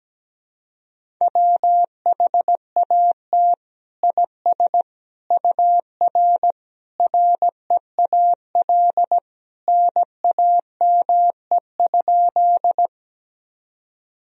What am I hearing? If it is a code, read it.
Morse code, heard as a keyed tone, about 17 wpm, 700 Hz. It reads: WHAT IS UR REAL NAME?
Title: Sentences from Top 300 Words and Abbreviations 17wpm